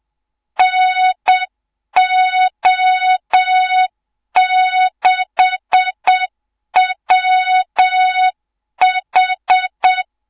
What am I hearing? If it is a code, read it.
Morse code, heard as a keyed tone, about 7 wpm, 3700 Hz, with noise. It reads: NO6WH
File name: Morze-HA2MN / hang / 24 GYAKORLAT